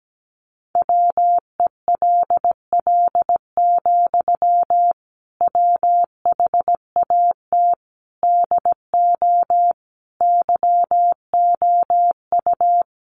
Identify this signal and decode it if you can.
Morse code, heard as a keyed tone, about 17 wpm, 700 Hz. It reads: WELL, WHAT DO YOU
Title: Sentences from Top 200 Words 17wpm